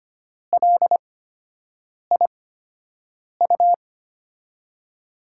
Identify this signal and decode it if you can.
Morse code, heard as a keyed tone, about 25 wpm, spaced as for 7 wpm, 700 Hz. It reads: LIU
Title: USA Call Signs - 6 Characters - 8x Character Spacing - 25wpm